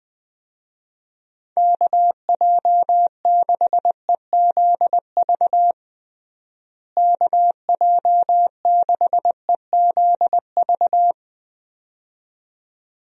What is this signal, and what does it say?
Morse code, heard as a keyed tone, about 21 wpm, 700 Hz. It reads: KJ6EZV KJ6EZV
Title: USA Call Signs - 6 Characters - Repeated 2x - 20wpm